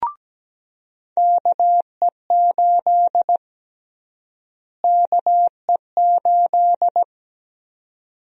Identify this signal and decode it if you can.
Morse code, heard as a keyed tone, about 17 wpm, 700 Hz. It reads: KE8 KE8
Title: USA Call Sign Prefixes - Repeated 2x - 17wpm